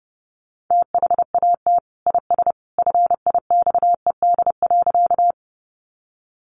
Morse code, heard as a keyed tone, about 30 wpm, 700 Hz. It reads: THAT IS FIXED.
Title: Sentences from Top 1100 Words 30wpm